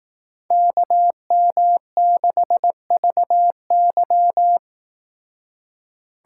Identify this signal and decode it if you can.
Morse code, heard as a keyed tone, about 18 wpm, 700 Hz. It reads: KM6VY
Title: Speed Racing - USA Call Signs 15wpm